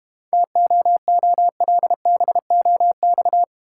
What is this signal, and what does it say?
Morse code, heard as a keyed tone, about 30 wpm, 700 Hz. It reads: TOOLBOX
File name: practice / 25wpm / Difficult Words - Speed-Racing with Hints - 25wpm